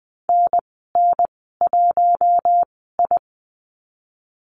Morse code, heard as a keyed tone, about 20 wpm, spaced as for 15 wpm, 700 Hz. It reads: NN1I
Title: 069 Lesson 12g - Letter d - Call Signs 20wpm